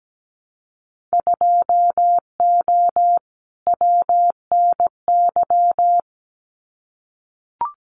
Morse code, heard as a keyed tone, about 17 wpm, 700 Hz. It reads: 2O WNY
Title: Winter Field Day Exchanges Encore 17wpm